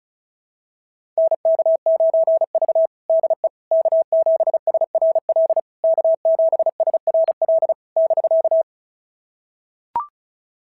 Morse code, heard as a keyed tone, about 36 wpm, 660 Hz. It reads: NK9V DE K7SRL K7SRL <BK>
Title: SOTA QSO Practice 35wpm